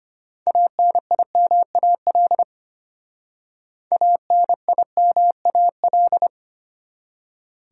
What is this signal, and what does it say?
Morse code, heard as a keyed tone, about 29 wpm, 700 Hz. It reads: ANIMAL ANIMAL
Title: Speed-Racing Top 500 Words 25wpm